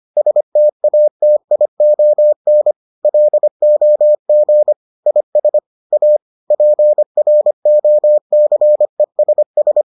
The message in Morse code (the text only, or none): STATION LOG IS A PROCESS